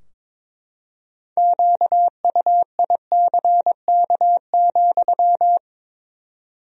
Morse code, heard as a keyed tone, about 22 wpm, 700 Hz. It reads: QUICK,
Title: 4x Word-Spacing - Sentences from Top 500 Words - 22wpm